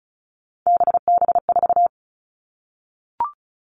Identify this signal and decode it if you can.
Morse code, heard as a keyed tone, about 34 wpm, 700 Hz. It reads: BB4
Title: All Cut Numbers - 3-Digit 35wpm